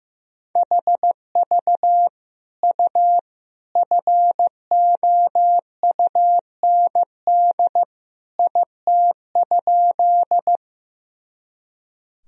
Morse code, heard as a keyed tone, about 15 wpm, 700 Hz. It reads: HV U FOUND IT?
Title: Sentences from Top 300 Words and Abbreviations 15wpm